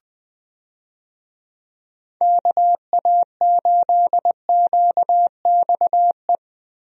Morse code, heard as a keyed tone, about 20 wpm, 700 Hz. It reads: KA8QXE
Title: USA Call Signs - 6 Characters - Repeated 3x - 20wpm